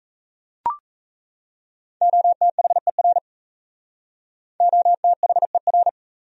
Morse code, heard as a keyed tone, about 37 wpm, 700 Hz. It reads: OTHER OTHER